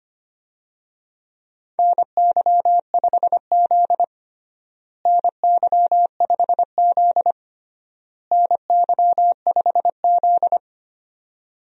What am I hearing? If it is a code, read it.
Morse code, heard as a keyed tone, about 25 wpm, 700 Hz. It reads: NY5Z NY5Z NY5Z
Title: USA Call Signs - 4 Characters - Repeated 3x - 25wpm